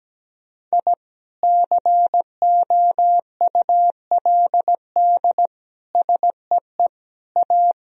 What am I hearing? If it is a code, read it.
Morse code, heard as a keyed tone, about 17 wpm, 700 Hz. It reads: I COULD SEE A
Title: Sentences from Top 1000 Words - Review Part 2 - 17wpm